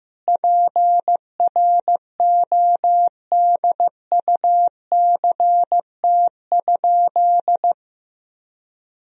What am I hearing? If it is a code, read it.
Morse code, heard as a keyed tone, about 15 wpm, 700 Hz. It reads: PRODUCT?
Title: Sentences from Top 1000 Words - Review Part 2 - 15wpm